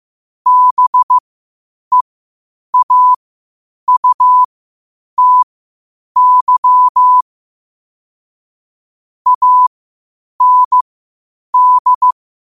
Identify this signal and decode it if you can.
Morse code, heard as a keyed tone, about 15 wpm, spaced as for 8 wpm, 1000 Hz. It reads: BEAUTY AND
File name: qotdfiles / 8wpm